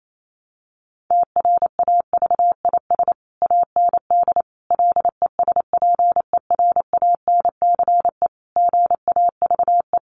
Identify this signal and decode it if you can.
Morse code, heard as a keyed tone, about 28 wpm, 700 Hz. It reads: TRAVIS AND LESPERANCE GAVE
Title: A Sound Of Thunder 28wpm